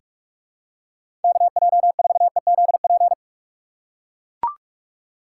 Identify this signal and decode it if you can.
Morse code, heard as a keyed tone, about 44 wpm, 700 Hz. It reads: KJ4EZP
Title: USA Call Signs - 6 Characters - 6x Character Spacing - 45wpm